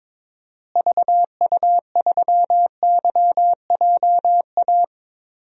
Morse code, heard as a keyed tone, about 22 wpm, 700 Hz. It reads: VU3YJA
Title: DXCC-100 Call Signs - 6 Characters - Repeated 2x - 22wpm